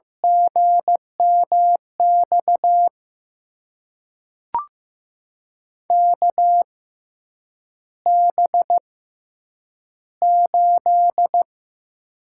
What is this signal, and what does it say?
Morse code, heard as a keyed tone, about 15 wpm, spaced as for 8 wpm, 700 Hz. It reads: GMX KB8
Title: USA Call Signs - 6 Characters - 6x Character Spacing - 15wpm